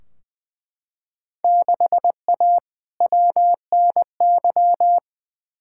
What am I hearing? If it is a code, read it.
Morse code, heard as a keyed tone, about 20 wpm, 700 Hz. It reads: 6A WNY
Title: Field Day Exchanges Encore - 20wpm